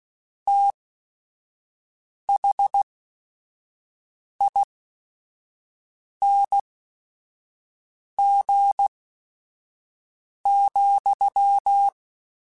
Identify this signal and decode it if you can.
Morse code, heard as a keyed tone, about 16 wpm, spaced as for 5 wpm, 800 Hz. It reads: THING,
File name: Special INGs / Special INGs-mp3s / VerySlowToFaster Ing @5